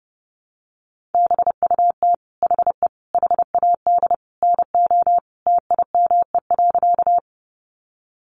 Morse code, heard as a keyed tone, about 30 wpm, 700 Hz. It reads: BUT HE HAD NO TIME.
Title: Sentences from Top 100 Words 30wpm